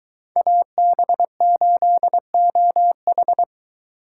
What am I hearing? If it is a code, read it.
Morse code, heard as a keyed tone, about 23 wpm, 700 Hz. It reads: AB8OH